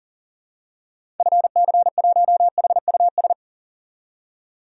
Morse code, heard as a keyed tone, about 40 wpm, 700 Hz. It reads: FC1HUS
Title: DXCC-100 Call Signs - 6 Characters - 40wpm